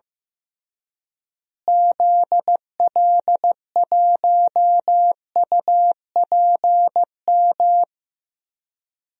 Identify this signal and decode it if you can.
Morse code, heard as a keyed tone, about 15 wpm, 700 Hz. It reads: ZL1UPM
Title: DXCC-100 Call Signs - 6 Characters - Repeated 3x - 15wpm